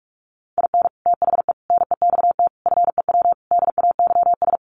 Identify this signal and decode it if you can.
Morse code, heard as a keyed tone, about 44 wpm, 700 Hz. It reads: IN THE NEXT FEW DAYS
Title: Sets of 5 Words Encore2 45wpm